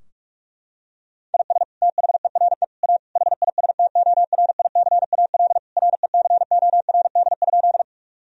Morse code, heard as a keyed tone, about 41 wpm, 700 Hz. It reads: IS THERE A HISTORICAL RECORD?
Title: Sentences from Top 1000 Words - Review Part 4 - 45wpm